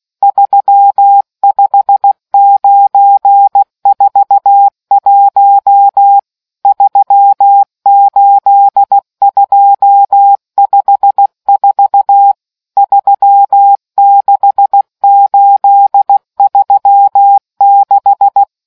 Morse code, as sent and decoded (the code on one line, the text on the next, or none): ...-- ..... ----. ....- .---- / ...-- ---.. ..--- ..... ....- / ...-- -.... ---.. ...-- -....
35941 38254 36836